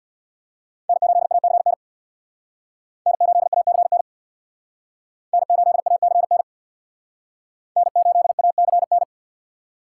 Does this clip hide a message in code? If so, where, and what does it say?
Yes: Morse code from 0.9 to 9.0 s, heard as a keyed tone, about 67 wpm, 700 Hz. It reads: N8ACN N8ACN N8ACN N8ACN